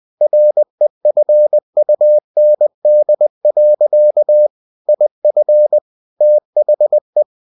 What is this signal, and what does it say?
Morse code, heard as a keyed tone, about 20 wpm, 600 Hz. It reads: REFUND. IF THE
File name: cw-books / winnie-the-pooh / winnie-20wpm-page-0119